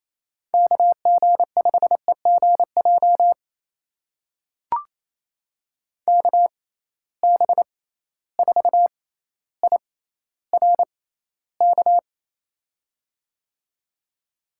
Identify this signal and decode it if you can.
Morse code, heard as a keyed tone, about 27 wpm, spaced as for 14 wpm, 700 Hz. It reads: KG5EGJ KB4IRK
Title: USA Call Signs - 6 Characters - 6x Character Spacing - 28wpm